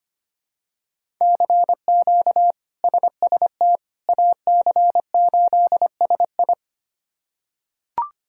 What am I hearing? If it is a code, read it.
Morse code, heard as a keyed tone, about 25 wpm, 700 Hz. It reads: CQ SST AC8SI
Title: K1USN SST QSO Practice - 25wpm